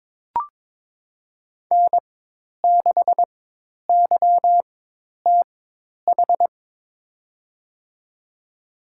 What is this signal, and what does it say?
Morse code, heard as a keyed tone, about 22 wpm, spaced as for 10 wpm, 700 Hz. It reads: N6YTH
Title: USA Call Signs - 5 Characters - 4x Character Spacing - 22wpm